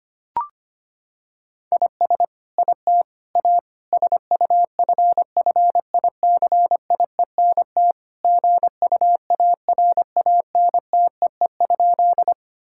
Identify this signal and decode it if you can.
Morse code, heard as a keyed tone, about 25 wpm, 700 Hz. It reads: IS IT A SUFFICIENT GUARANTEE?
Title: Sentences from Top 1600 Words 25wpm